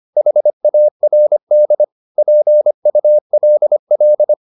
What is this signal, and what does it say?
Morse code, heard as a keyed tone, about 25 wpm, 600 Hz. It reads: HARD PULL
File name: the old man and the sea part 1 25wpm